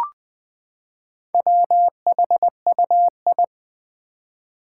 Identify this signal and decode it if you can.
Morse code, heard as a keyed tone, about 21 wpm, 700 Hz. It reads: WHUI
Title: Four Challenging Letter-Number Combinations 20wpm